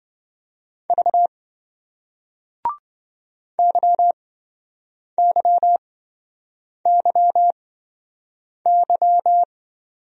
Morse code, heard as a keyed tone, about 26 wpm, spaced as for 7 wpm, 700 Hz. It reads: V YYYY